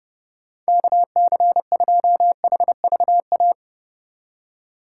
Morse code, heard as a keyed tone, about 30 wpm, 700 Hz. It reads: KC2HVA
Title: USA Call Signs - 6 Characters - 3x Character Spacing - 30wpm